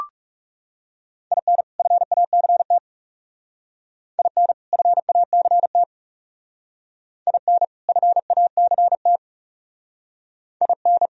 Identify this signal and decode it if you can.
Morse code, heard as a keyed tone, about 38 wpm, 700 Hz. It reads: IN FACT IN FACT IN FACT IN